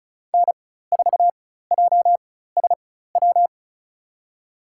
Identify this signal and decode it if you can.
Morse code, heard as a keyed tone, about 36 wpm, spaced as for 16 wpm, 700 Hz. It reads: N4JSW
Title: USA Call Signs - 5 Characters - 4x Character Spacing - 35wpm